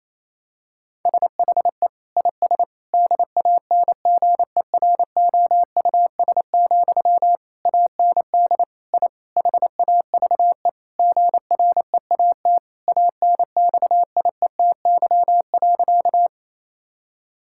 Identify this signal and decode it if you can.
Morse code, heard as a keyed tone, about 28 wpm, 700 Hz. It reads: SHE IS DANGEROUS, AND I HAVE GREAT ANXIETY.
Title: Sentences from Top 1600 Words 28wpm